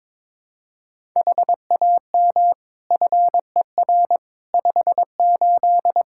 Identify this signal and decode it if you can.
Morse code, heard as a keyed tone, about 21 wpm, 700 Hz. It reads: HAM FER 58
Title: Practice QSO - Exchange 3 - More Info - 22wpm